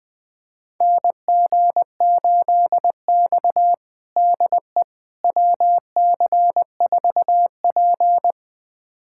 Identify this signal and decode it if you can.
Morse code, heard as a keyed tone, about 21 wpm, 700 Hz. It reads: NG8X DE WC4P
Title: USA Callsign DE Callsign - 4 Characters - 20wpm